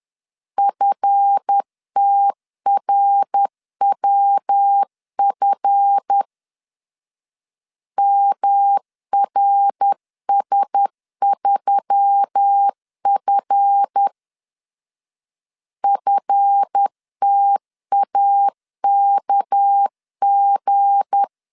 Morse code, heard as a keyed tone, about 10 wpm, 800 Hz. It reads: FTRWF MRS3F FTAKG
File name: LEZIONI / 7 GF9Z / 7 GF 10wpm 2514